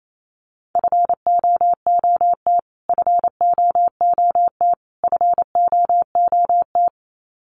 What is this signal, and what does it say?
Morse code, heard as a keyed tone, about 27 wpm, 700 Hz. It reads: FOOT FOOT FOOT
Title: Top 500 Words Repeated 28wpm